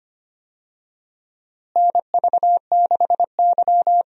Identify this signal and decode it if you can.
Morse code, heard as a keyed tone, about 25 wpm, 700 Hz. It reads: NV6Y
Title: USA Call Signs - 4 Characters - Repeated 3x - 25wpm